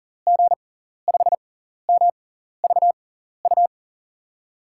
Morse code, heard as a keyed tone, about 40 wpm, spaced as for 14 wpm, 700 Hz. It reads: G5MVU
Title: USA Call Signs - 6 Characters - 6x Character Spacing - 40wpm